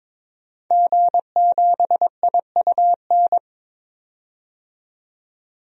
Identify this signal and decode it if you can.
Morse code, heard as a keyed tone, about 22 wpm, 700 Hz. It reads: G7IUN
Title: DXCC-100 Call Signs - 5 Characters - 22wpm